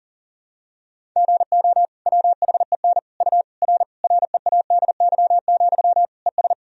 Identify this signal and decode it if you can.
Morse code, heard as a keyed tone, about 37 wpm, 700 Hz. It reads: GO WHEN U R READY, ES
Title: Sentences from Top 500 Words and Abbreviations 40wpm